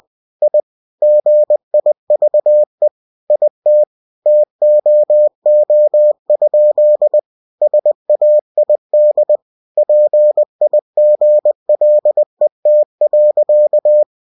Morse code, heard as a keyed tone, about 20 wpm, 600 Hz. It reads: I GIVE IT TOO? SAID PIGLET.
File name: cw-books / winnie-the-pooh / winnie-20wpm-page-0049